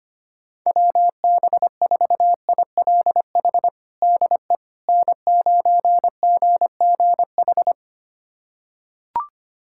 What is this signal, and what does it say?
Morse code, heard as a keyed tone, about 25 wpm, 700 Hz. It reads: WB4ILH DE N9GGH
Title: USA Callsign DE Callsign 25wpm